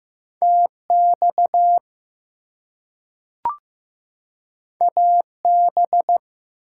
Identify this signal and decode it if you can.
Morse code, heard as a keyed tone, about 15 wpm, 700 Hz. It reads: TX AB
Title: US States and Canadian Provinces 15wpm